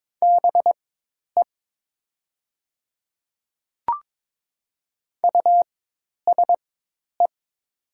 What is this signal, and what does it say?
Morse code, heard as a keyed tone, about 22 wpm, spaced as for 10 wpm, 700 Hz. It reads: BE USE
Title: 4x Character Spacing - Top 300 Words - 22wpm